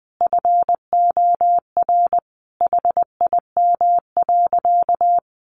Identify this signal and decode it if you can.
Morse code, heard as a keyed tone, about 20 wpm, 700 Hz. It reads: FOR HIM.
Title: Sentences from Top 1500 Words 20wpm